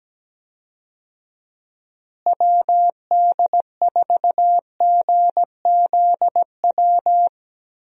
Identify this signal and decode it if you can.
Morse code, heard as a keyed tone, about 17 wpm, 700 Hz. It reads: WD4GZW
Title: USA Call Signs - 6 Characters - Repeated 3x - 17wpm